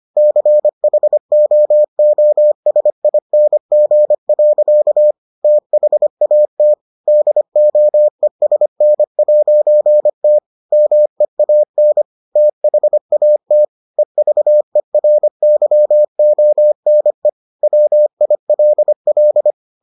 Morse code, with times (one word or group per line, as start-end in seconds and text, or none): 0.2-5.1 CHOOSING.
5.4-6.7 THAT
7.1-10.4 DOESN'T
10.7-12.0 MEAN
12.4-13.6 THAT
14.0-17.3 EVERYONE
17.6-19.5 WILL